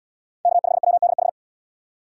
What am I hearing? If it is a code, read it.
Morse code, heard as a keyed tone, about 75 wpm, 700 Hz. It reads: K5WDH